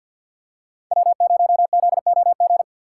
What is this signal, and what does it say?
Morse code, heard as a keyed tone, about 47 wpm, 700 Hz. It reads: W0ZOG
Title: USA Call Signs - 5 Characters - 6x Character Spacing - 50wpm